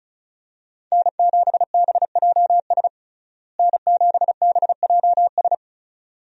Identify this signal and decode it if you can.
Morse code, heard as a keyed tone, about 34 wpm, 700 Hz. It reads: N7BJS N7BJS
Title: USA Call Signs - 5 Characters - Repeated 2x - 35wpm